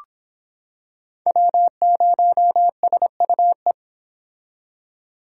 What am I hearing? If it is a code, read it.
Morse code, heard as a keyed tone, about 26 wpm, 700 Hz. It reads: W0SUE